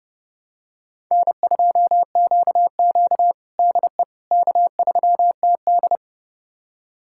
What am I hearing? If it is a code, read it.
Morse code, heard as a keyed tone, about 30 wpm, 700 Hz. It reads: N2QQ DE K3TD
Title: USA Callsign DE Callsign - 4 Characters - 30wpm